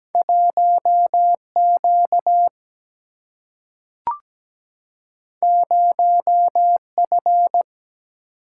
Morse code, heard as a keyed tone, about 17 wpm, 700 Hz. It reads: 1Q 0F
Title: Single-Letter Single-Number 17wpm